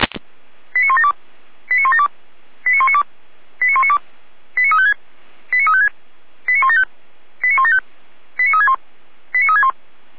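Five-tone selective call (ZVEI), heard as quick runs of five tones, tones of about 70 ms. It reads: 89172 89172 89192 89192 89356 89356 89175 89175 89361 89361
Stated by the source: ZVEI five-tone selective call 89560 received by SDR.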